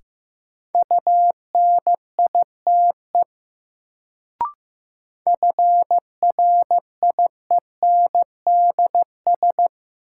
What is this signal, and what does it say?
Morse code, heard as a keyed tone, about 15 wpm, 700 Hz. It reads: UNITE FRIENDS
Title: Sentences from Top 1000 Words - Review Part 5 - New Word Drill - 15wpm